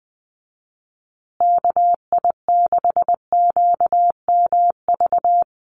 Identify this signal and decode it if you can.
Morse code, heard as a keyed tone, about 20 wpm, 700 Hz. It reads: KI6QMV
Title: USA Call Signs Repeated 3x 20wpm